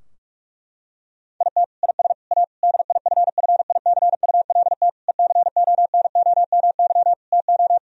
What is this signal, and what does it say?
Morse code, heard as a keyed tone, about 46 wpm, 700 Hz. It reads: IT IS A DIFFICULT ECONOMY TO
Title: Sentences from Top 1000 Words - Review Part 3 - 45wpm